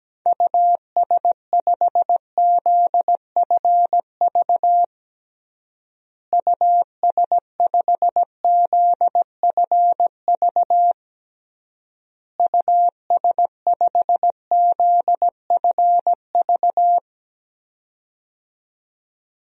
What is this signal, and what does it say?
Morse code, heard as a keyed tone, about 17 wpm, 700 Hz. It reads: US5ZFV US5ZFV US5ZFV
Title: DXCC-100 Call Signs - 6 Characters - Repeated 3x - 17wpm